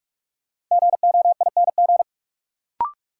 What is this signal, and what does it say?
Morse code, heard as a keyed tone, about 44 wpm, 700 Hz. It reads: GOING
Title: Sets of 2 Words - New Word Drill - 45wpm